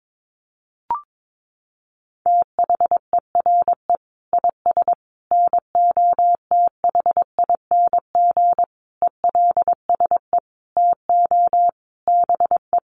Morse code, heard as a keyed tone, about 22 wpm, 700 Hz. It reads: THERE IS NOTHING ELSE TO BE
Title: Sentences from Top 800 Words 22wpm